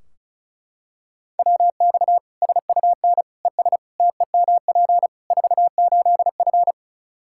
Morse code, heard as a keyed tone, about 36 wpm, 700 Hz. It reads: WX SUN ES TEMP 48F
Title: K1USN SST QSO Practice 35wpm